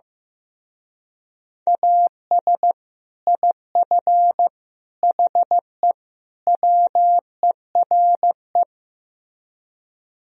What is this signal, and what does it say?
Morse code, heard as a keyed tone, about 15 wpm, 700 Hz. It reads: AS IF HE WERE
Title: Speed-Racing - Sets of 4 Words 15wpm